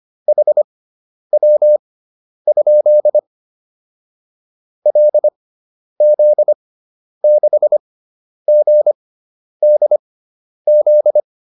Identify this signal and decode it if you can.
Morse code, heard as a keyed tone, about 25 wpm, spaced as for 10 wpm, 600 Hz. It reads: HW? LZ6GDZ
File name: QSO407 10